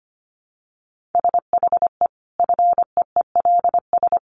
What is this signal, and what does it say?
Morse code, heard as a keyed tone, about 25 wpm, 700 Hz. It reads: SHE FEELS
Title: Additional Easy Phrases 25wpm